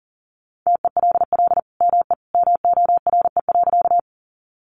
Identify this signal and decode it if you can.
Morse code, heard as a keyed tone, about 40 wpm, 700 Hz. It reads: TELL ME MORE.
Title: Sentences from Top 300 Words 40wpm